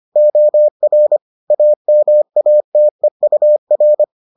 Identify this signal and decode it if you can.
Morse code, heard as a keyed tone, about 25 wpm, 600 Hz. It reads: OR AMATEUR